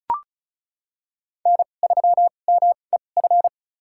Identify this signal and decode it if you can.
Morse code, heard as a keyed tone, about 34 wpm, spaced as for 25 wpm, 700 Hz. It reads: N3MEF
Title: USA Call Signs - 5 Characters - 2x Character Spacing - 35wpm